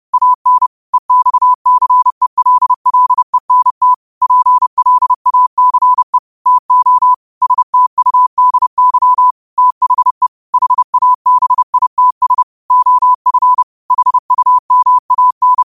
Morse code, heard as a keyed tone, about 30 wpm, 1000 Hz. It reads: AN EXCELLENT PLACE TO STUDY THE HABITS OF HUMAN
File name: qotdfiles / 30wpm